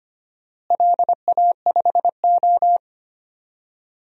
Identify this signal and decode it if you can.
Morse code, heard as a keyed tone, about 25 wpm, 700 Hz. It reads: LA5O
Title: DXCC-100 Call Signs - 4 Characters - Repeated 3x - 25wpm